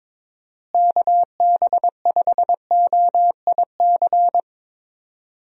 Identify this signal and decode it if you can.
Morse code, heard as a keyed tone, about 22 wpm, 700 Hz. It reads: KB5OIC
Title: USA Call Signs - 6 Characters - Repeated 2x - 22wpm